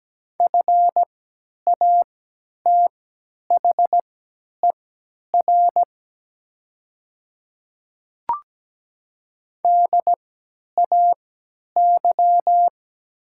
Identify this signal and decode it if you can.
Morse code, heard as a keyed tone, about 17 wpm, spaced as for 10 wpm, 700 Hz. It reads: FATHER DAY
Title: 3x Character Spacing - Top 300 Words - 17wpm